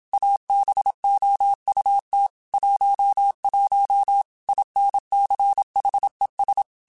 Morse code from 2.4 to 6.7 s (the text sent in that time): 11 INCHES